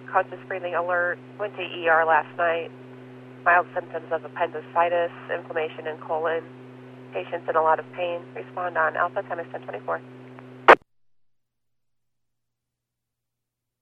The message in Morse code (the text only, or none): none